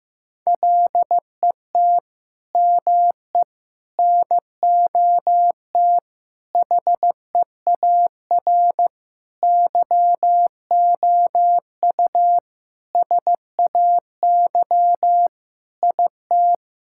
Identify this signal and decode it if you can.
Morse code, heard as a keyed tone, about 15 wpm, 700 Hz. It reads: LET ME NOT HEAR YOU SAY IT
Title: Sentences from Top 1600 Words 15wpm